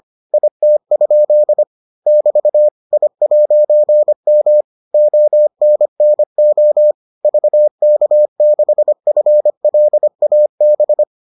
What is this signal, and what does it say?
Morse code, heard as a keyed tone, about 25 wpm, 600 Hz. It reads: IT? <BT> I'M ONNO VK6FLAB